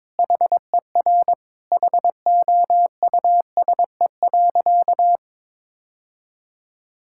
Morse code, heard as a keyed tone, about 22 wpm, 700 Hz. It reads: HER HOUSE.
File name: practice / 22wpm / Sentences with 4-Word Phrases - 22wpm